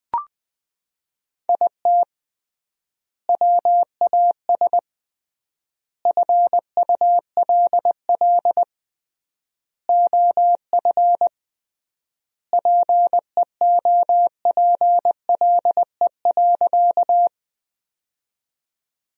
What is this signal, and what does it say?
Morse code, heard as a keyed tone, about 21 wpm, 700 Hz. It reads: IT WAS FULL OF PEOPLE.
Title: 3x Word-Spacing - Sentences from Top 500 Words - 20wpm